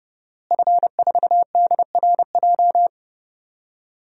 Morse code, heard as a keyed tone, about 30 wpm, 700 Hz. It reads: F4DRJ
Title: DXCC-100 Call Signs - 5 Characters - Repeated 3x - 30wpm